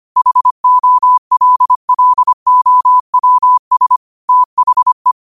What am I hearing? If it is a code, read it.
Morse code, heard as a keyed tone, about 25 wpm, 1000 Hz. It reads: HOLLOWS THE
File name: qotdfiles / 25wpm